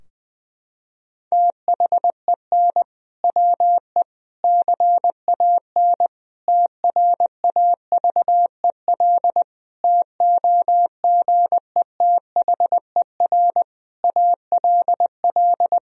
Morse code, heard as a keyed tone, about 20 wpm, 700 Hz. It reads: THEN WE CAN TRAVEL TOGETHER ALL
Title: Sentences from Top 1000 Words - Review Part 2 - 20wpm